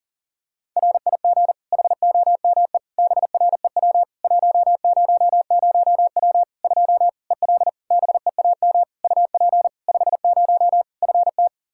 Morse code, heard as a keyed tone, about 37 wpm, 700 Hz. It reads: RIG HOME BREW 100W 2 EL BEAM UP 50 FT
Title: Practice QSO - Exchange 2 - Info - 40wpm